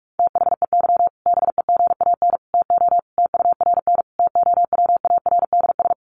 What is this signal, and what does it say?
Morse code, heard as a keyed tone, about 44 wpm, 700 Hz. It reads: THEY BEGAN TO TURN TOWARDS
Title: Sentences from Top 1100 Words 45wpm